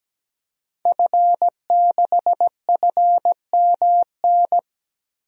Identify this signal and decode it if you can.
Morse code, heard as a keyed tone, about 17 wpm, 700 Hz. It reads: F6FMN
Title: DXCC-100 Call Signs - 5 Characters - Repeated 3x - 17wpm